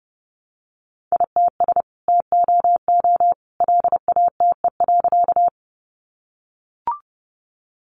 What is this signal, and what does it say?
Morse code, heard as a keyed tone, about 30 wpm, 700 Hz. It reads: ITS TOO LATE.
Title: Sentences from Top 600 Words 30wpm